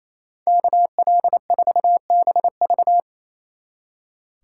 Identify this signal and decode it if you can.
Morse code, heard as a keyed tone, about 35 wpm, 700 Hz. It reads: KL4BV
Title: Speed Racing - USA 2x2 Call Signs 28wpm